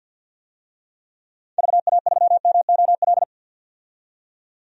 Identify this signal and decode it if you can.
Morse code, heard as a keyed tone, about 51 wpm, 700 Hz. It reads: VA3MOL
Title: DXCC-100 Call Signs - 6 Characters - 50wpm